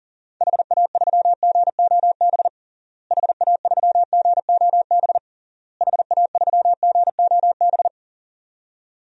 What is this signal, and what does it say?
Morse code, heard as a keyed tone, about 40 wpm, 700 Hz. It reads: HA3GOB HA3GOB HA3GOB
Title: DXCC-100 Call Signs - 6 Characters - Repeated 3x - 40wpm